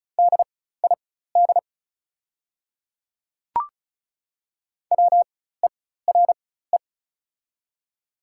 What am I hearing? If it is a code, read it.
Morse code, heard as a keyed tone, about 35 wpm, spaced as for 16 wpm, 690 Hz. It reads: DID WERE
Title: 4x Character Spacing - Top 100 Words - 35wpm